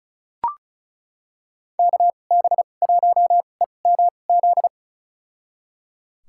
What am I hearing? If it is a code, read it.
Morse code, heard as a keyed tone, about 36 wpm, spaced as for 25 wpm, 700 Hz. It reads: KB1EMZ
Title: USA Call Signs - 6 Characters - 2x Character Spacing - 35wpm